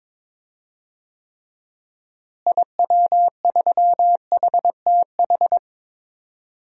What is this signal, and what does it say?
Morse code, heard as a keyed tone, about 22 wpm, 700 Hz. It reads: IW3HTH
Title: DXCC-100 Call Signs - 6 Characters - Repeated 3x - 22wpm